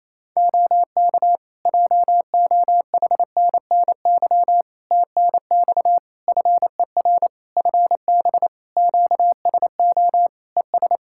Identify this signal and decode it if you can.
Morse code, heard as a keyed tone, about 28 wpm, 700 Hz. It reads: OK JOHNNY TNX FER FB QSO ES
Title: Practice QSO - Exchange 4 - Ending - 28wpm